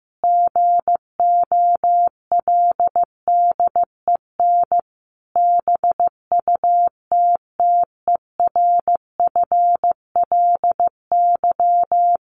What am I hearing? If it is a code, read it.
Morse code, heard as a keyed tone, about 15 wpm, 700 Hz. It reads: GOLDEN BUTTERFLY
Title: A Sound Of Thunder 15wpm